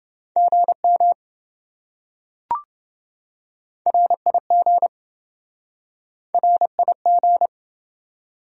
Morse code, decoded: GM RIG RIG

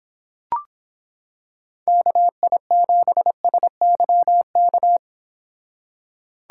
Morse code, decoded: KI7SYK

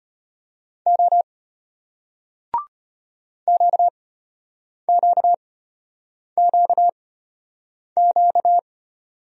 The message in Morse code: O QQQQ

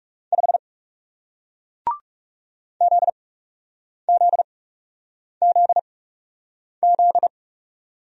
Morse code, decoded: 5 ZZZZ

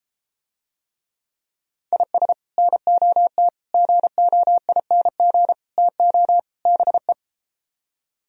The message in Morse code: IS NOT GOING TO BE

